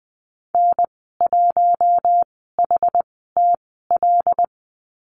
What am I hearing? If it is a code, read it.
Morse code, heard as a keyed tone, about 20 wpm, spaced as for 15 wpm, 700 Hz. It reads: N1HTL
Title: 122 Lesson 20g - Number 3 - Call Signs 20wpm